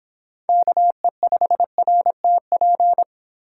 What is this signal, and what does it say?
Morse code, heard as a keyed tone, about 31 wpm, 700 Hz. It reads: KE5RTP